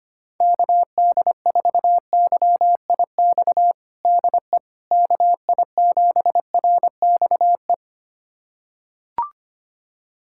Morse code, heard as a keyed tone, about 25 wpm, 700 Hz. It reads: KD4YIX DE KI7RXE
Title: USA Callsign DE Callsign - 6 Characters - 25wpm